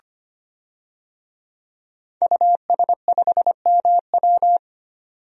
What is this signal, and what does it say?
Morse code, heard as a keyed tone, about 25 wpm, 700 Hz. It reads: US5MW
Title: DXCC-100 Call Signs - 5 Characters - 25wpm